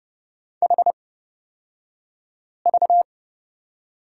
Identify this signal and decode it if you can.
Morse code, heard as a keyed tone, about 30 wpm, spaced as for 5 wpm, 700 Hz. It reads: HV
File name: Lesson 37e - Letter x - Three Characters 30wpm